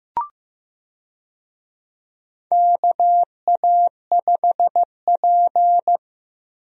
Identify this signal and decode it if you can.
Morse code, heard as a keyed tone, about 15 wpm, 700 Hz. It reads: KA5P
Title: USA Call Signs - 4 Characters - Repeated 2x - 15wpm